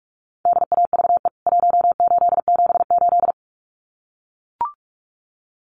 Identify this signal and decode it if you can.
Morse code, heard as a keyed tone, about 44 wpm, 700 Hz. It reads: DAVE 1878
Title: CWT Exchange 45wpm